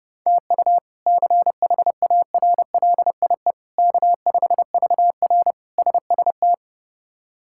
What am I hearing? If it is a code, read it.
Morse code, heard as a keyed tone, about 32 wpm, 700 Hz. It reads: TU CHARLIE K5VR SST
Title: K1USN SST QSO Practice - 30wpm